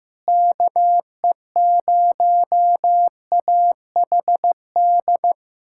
Morse code, heard as a keyed tone, about 16 wpm, 700 Hz. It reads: KE0AHD
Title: USA Callsign DE Callsign 15wpm